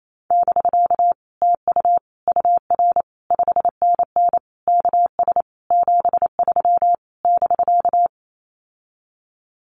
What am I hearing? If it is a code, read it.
Morse code, heard as a keyed tone, about 28 wpm, 700 Hz. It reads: <BK> TU UR 5NN KS 73 <BK>
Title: Common QSO Phrases Encore 28wpm